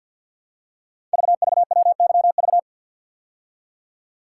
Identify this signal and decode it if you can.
Morse code, heard as a keyed tone, about 48 wpm, 690 Hz. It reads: VVWYV
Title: Code Groups - 5 Letters - 50wpm